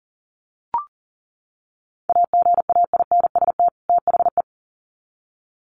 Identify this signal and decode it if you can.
Morse code, heard as a keyed tone, about 40 wpm, 700 Hz. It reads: AGAINST THE
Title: Sets of 2 Words 40wpm